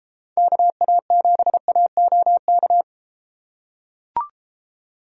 Morse code, heard as a keyed tone, about 31 wpm, 700 Hz. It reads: KA7AOK